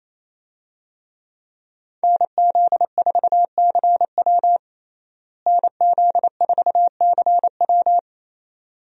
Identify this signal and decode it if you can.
Morse code, heard as a keyed tone, about 28 wpm, 700 Hz. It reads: NZ4CW NZ4CW
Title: USA Call Signs - 5 Characters - Repeated 2x - 28wpm